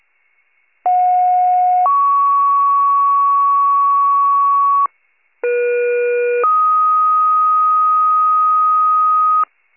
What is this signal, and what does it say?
Two-tone page, heard as a one-second tone, then a three-second tone, about 730 then 1120 Hz, about 480 then 1250 Hz. It is heard from 0.8 to 9.4 s.
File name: pager / 56 RESCUE / wcalert